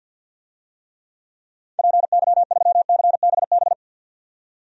Morse code, heard as a keyed tone, about 52 wpm, 700 Hz. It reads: PY3CBB
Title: DXCC-100 Call Signs - 6 Characters - 50wpm